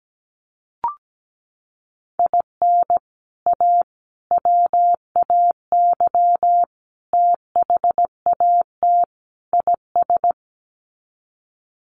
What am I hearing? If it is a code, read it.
Morse code, heard as a keyed tone, about 17 wpm, 700 Hz. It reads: IN A WAY THAT IS
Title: Sets of 5 Words Encore 17wpm